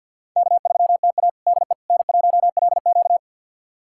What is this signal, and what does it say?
Morse code, heard as a keyed tone, about 50 wpm, 700 Hz. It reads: K3TA DE N1LQ
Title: USA Callsign DE Callsign - 4 Characters - 50wpm